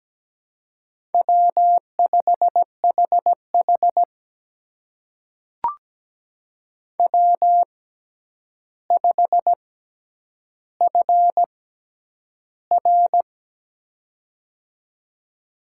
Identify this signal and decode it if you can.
Morse code, heard as a keyed tone, about 17 wpm, spaced as for 9 wpm, 700 Hz. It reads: W5HH W5FR
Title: USA Call Signs - 4 Characters - 6x Character Spacing - 17wpm